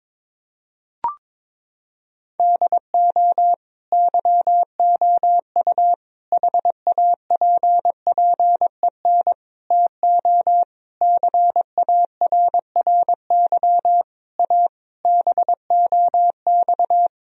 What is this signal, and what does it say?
Morse code, heard as a keyed tone, about 22 wpm, 700 Hz. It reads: DO YOU HAPPEN TO CARRY A BOX
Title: Sentences from Top 1000 Words - Review Part 2 - 22wpm